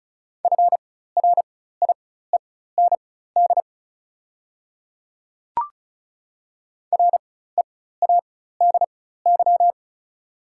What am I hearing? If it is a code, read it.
Morse code, heard as a keyed tone, about 36 wpm, spaced as for 16 wpm, 700 Hz. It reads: FRIEND READY
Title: 4x Character Spacing - Top 500 Words - 35wpm